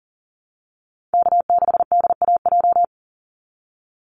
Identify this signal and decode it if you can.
Morse code, heard as a keyed tone, about 40 wpm, 700 Hz. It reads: K6DAJ
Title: Call Signs Repeated 2x - 40wpm